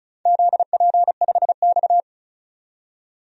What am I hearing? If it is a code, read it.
Morse code, heard as a keyed tone, about 36 wpm, 700 Hz. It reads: ZP5X
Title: DXCC-100 Call Signs - 4 Characters - 35wpm